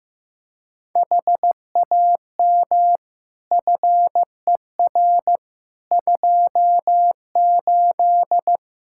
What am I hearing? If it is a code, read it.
Morse code, heard as a keyed tone, about 15 wpm, 700 Hz. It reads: HAM FER 28
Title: Practice QSO - Exchange 3 - More Info - 15wpm